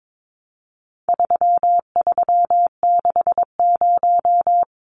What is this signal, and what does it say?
Morse code, heard as a keyed tone, about 22 wpm, 700 Hz. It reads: 3360
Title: Numbers - 4-Digit 22wpm